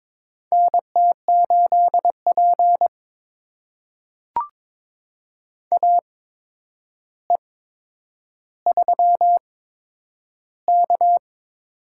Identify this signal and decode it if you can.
Morse code, heard as a keyed tone, about 22 wpm, spaced as for 9 wpm, 700 Hz. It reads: NT8P AE3K
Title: USA Call Signs - 4 Characters - 8x Character Spacing - 22wpm